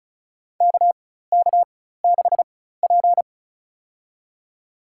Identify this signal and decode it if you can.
Morse code, heard as a keyed tone, about 35 wpm, spaced as for 16 wpm, 700 Hz. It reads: KK6P
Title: USA Call Signs - 4 Characters - 4x Character Spacing - 35wpm